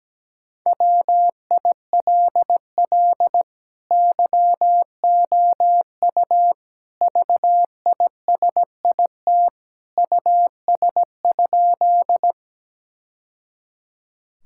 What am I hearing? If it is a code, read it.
Morse code, heard as a keyed tone, about 17 wpm, 700 Hz. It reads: WILL YOU VISIT US?
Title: Sentences from Top 1000 Words - Review Part 6 - 17wpm